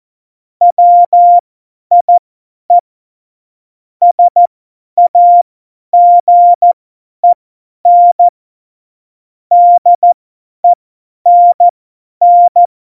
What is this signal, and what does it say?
Morse code, heard as a keyed tone, about 14 wpm, spaced as for 10 wpm, 700 Hz. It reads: WIE SAGEN DENN
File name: Morsekurs / Witze / Witz 21 Juli 2020, 50 F70